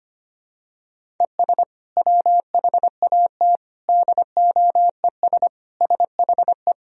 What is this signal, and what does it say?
Morse code, heard as a keyed tone, about 25 wpm, 700 Hz. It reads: ES WHAT DOES SHE
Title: Sentences from Top 400 Words and Abbreviations 25wpm